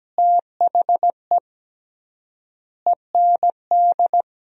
Morse code, heard as a keyed tone, about 17 wpm, 700 Hz. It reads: THE END
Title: 3x Word-Spacing - Sentences from Top 300 Words - 17wpm